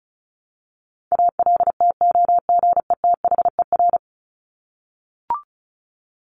Morse code, heard as a keyed tone, about 34 wpm, 700 Hz. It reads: ALTOGETHER
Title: Top 1501-1750 Words 35wpm